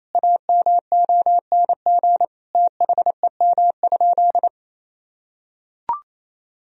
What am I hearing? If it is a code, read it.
Morse code, heard as a keyed tone, about 28 wpm, 700 Hz. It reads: AMONG THEM?
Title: Sentences from Top 1000 Words - Review Part 2 - 28wpm